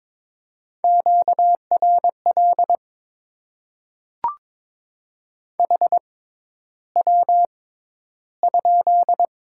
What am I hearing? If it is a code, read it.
Morse code, heard as a keyed tone, about 22 wpm, spaced as for 11 wpm, 700 Hz. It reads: QRL HW?
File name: Top 20 QSO Elements - 6x Character Spacing - 22wpm